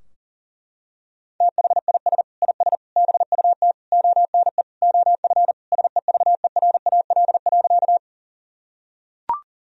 Morse code, heard as a keyed tone, about 37 wpm, 700 Hz. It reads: THIS IS BUT ONE OF SEVERAL.
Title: Sentences from Top 1000 Words - Review Part 2 - 40wpm